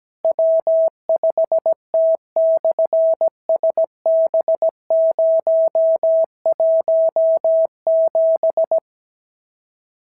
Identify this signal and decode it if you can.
Morse code, heard as a keyed tone, about 17 wpm, 650 Hz. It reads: W5T/SB017
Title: SOTA QSO Practice 17wpm